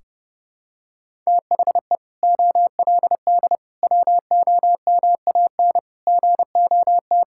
Morse code, heard as a keyed tone, about 30 wpm, 700 Hz. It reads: THE OLD WOMAN GOT
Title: Sentences from Top 1600 Words 30wpm